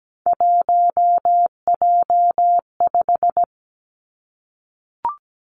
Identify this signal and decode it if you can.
Morse code, heard as a keyed tone, about 17 wpm, 700 Hz. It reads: 1J5
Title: Three Challenging Letter-Number Combinations 17wpm